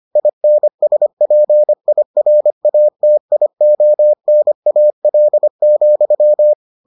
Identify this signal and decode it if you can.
Morse code, heard as a keyed tone, about 25 wpm, 600 Hz. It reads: INSPIRATIONAL,